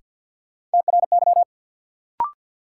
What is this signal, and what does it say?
Morse code, heard as a keyed tone, about 46 wpm, 700 Hz. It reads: TRY
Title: Sets of 2 Words - New Word Drill - 50wpm